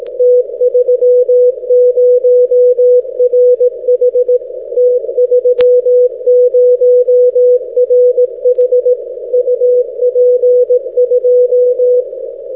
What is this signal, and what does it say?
Morse code, heard as a keyed tone, about 18 wpm, 500 Hz. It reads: T30RH T30RH UP 2
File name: t30rh- -x